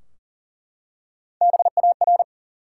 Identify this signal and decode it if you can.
Morse code, heard as a keyed tone, about 38 wpm, 700 Hz. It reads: BAR
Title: Three Letter Words - 40wpm